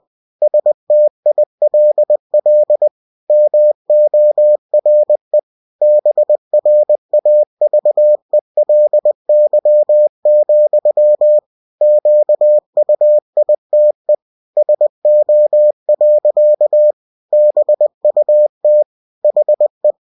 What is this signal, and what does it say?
Morse code, heard as a keyed tone, about 20 wpm, 600 Hz. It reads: STILL MORE BRAVELY, QUITE SO. BUT HE
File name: cw-books / winnie-the-pooh / winnie-20wpm-page-0040